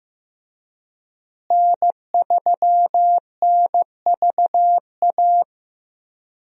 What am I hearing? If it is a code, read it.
Morse code, heard as a keyed tone, about 15 wpm, 700 Hz. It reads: N3NVA
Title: USA Call Signs - 5 Characters - Repeated 3x - 15wpm